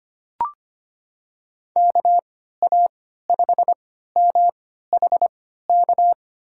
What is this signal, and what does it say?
Morse code, heard as a keyed tone, about 25 wpm, spaced as for 14 wpm, 700 Hz. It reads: KA5MHK
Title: USA Call Signs - 6 Characters - 3x Character Spacing - 25wpm